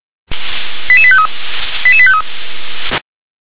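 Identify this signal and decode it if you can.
Five-tone selective call (ZVEI), heard as quick runs of five tones, tones of about 70 ms. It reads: 88853 88853
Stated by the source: ZVEI five-tone selective call 88714 received on SDR.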